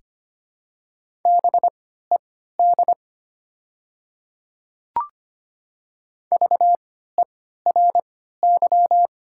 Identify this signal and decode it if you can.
Morse code, heard as a keyed tone, about 25 wpm, spaced as for 14 wpm, 700 Hz. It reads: BED VERY
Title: 3x Character Spacing - Top 500 Words - 25wpm